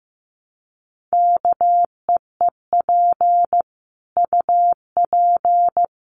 Morse code, heard as a keyed tone, about 15 wpm, 700 Hz. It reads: KEEP UP
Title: Sentences from Top 1300 Words 15wpm